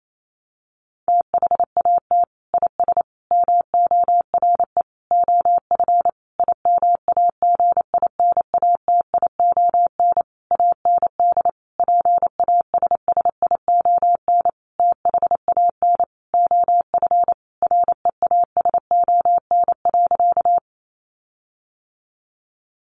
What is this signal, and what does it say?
Morse code, heard as a keyed tone, about 28 wpm, 700 Hz. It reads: THAT IS MORE OF IMAGINATION AND PASSION THAN OF REASON.
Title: Sentences from Top 1300 Words 28wpm